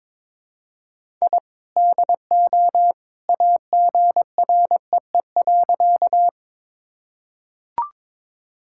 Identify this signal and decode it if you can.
Morse code, heard as a keyed tone, about 22 wpm, 700 Hz. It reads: I DO AGREE.
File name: practice / 22wpm / Sentences with 2-Word Phrases - 22wpm